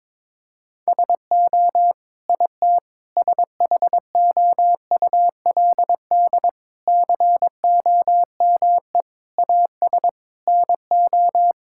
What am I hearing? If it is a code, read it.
Morse code, heard as a keyed tone, about 22 wpm, 700 Hz. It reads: SO IT SHOULD COME AS NO
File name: practice / 22wpm / Sets of 8 Words - 22wpm